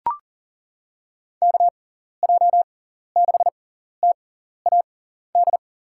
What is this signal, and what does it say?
Morse code, heard as a keyed tone, about 40 wpm, spaced as for 14 wpm, 700 Hz. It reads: KJ6TAD
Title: USA Call Signs - 6 Characters - 6x Character Spacing - 40wpm